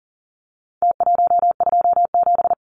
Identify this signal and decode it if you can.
Morse code, heard as a keyed tone, about 40 wpm, 700 Hz. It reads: T127
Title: Numbers - Contest Style - 4-Digit 40wpm